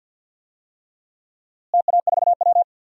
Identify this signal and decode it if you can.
Morse code, heard as a keyed tone, about 51 wpm, 700 Hz. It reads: TA4W
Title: DXCC-100 Call Signs - 4 Characters - Repeated 3x - 50wpm